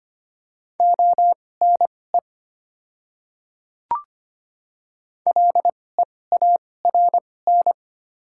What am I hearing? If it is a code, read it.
Morse code, heard as a keyed tone, about 25 wpm, spaced as for 18 wpm, 690 Hz. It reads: ONE LEARN
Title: 2x Character Spacing - Top 300 Words - 25wpm